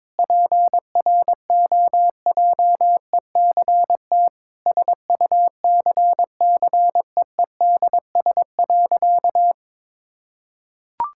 Morse code, heard as a keyed tone, about 22 wpm, 700 Hz. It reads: PROJECT SUCCEEDS.
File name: practice / 22wpm / Sentences with 8-Word Phrases - 22wpm